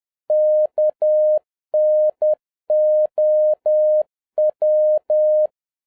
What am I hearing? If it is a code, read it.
Morse code, heard as a keyed tone, about 10 wpm, 610 Hz. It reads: KNOW